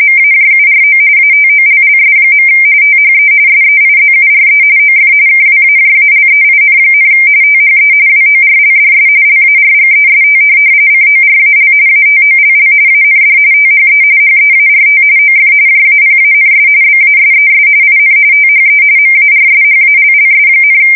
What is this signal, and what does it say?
RTTY, about 45.45 baud. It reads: MAJURO ATOLLS. THE SEIZURE OF KWAJALEIN, THE WORLDS LARGEST CORAL ATOLL AND THE HEART OF JAPANS BASES IN THE MARSHALLS,
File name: fsk170 / USS-New-Jersey